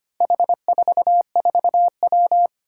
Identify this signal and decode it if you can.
Morse code, heard as a keyed tone, about 25 wpm, 700 Hz. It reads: H44W
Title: Four Challenging Letter-Number Combinations 25wpm